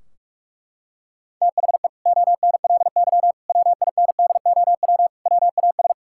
Morse code, heard as a keyed tone, about 42 wpm, 700 Hz. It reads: THE ONLY WINDOW WAS
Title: Sentences from Top 1600 Words 45wpm